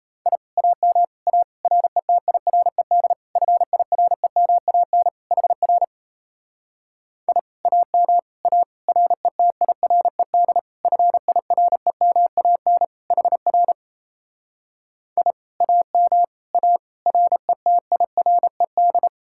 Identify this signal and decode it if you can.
Morse code, heard as a keyed tone, about 37 wpm, 700 Hz. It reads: I AM A RETIRED FIREMAN HR I AM A RETIRED FIREMAN HR I AM A RETIRED